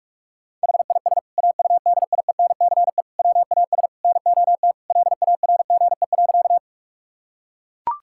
Morse code, heard as a keyed tone, about 42 wpm, 700 Hz. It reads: HIS AUDIENCE WAS NOT LARGE.
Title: Sentences from Top 1000 Words - Review Part 5 - 45wpm